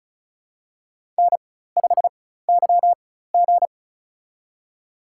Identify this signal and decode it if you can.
Morse code, heard as a keyed tone, about 34 wpm, spaced as for 16 wpm, 700 Hz. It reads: N5YG
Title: USA Call Signs - 4x Character Spacing - 35wpm